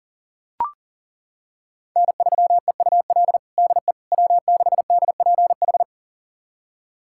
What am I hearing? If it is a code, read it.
Morse code, heard as a keyed tone, about 40 wpm, 700 Hz. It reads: N3EUL DE W6DPH
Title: USA Callsign DE Callsign - 5 Characters - 40wpm